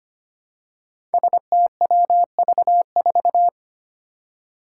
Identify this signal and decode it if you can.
Morse code, heard as a keyed tone, about 25 wpm, 700 Hz. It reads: STWV4
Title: Five Challenging Letter-Number Combinations 25wpm